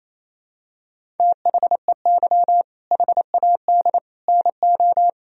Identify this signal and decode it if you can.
Morse code, heard as a keyed tone, about 27 wpm, 700 Hz. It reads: THEY HAD NO
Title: Sentences from Top 1000 Words - Review Part 1 - 28wpm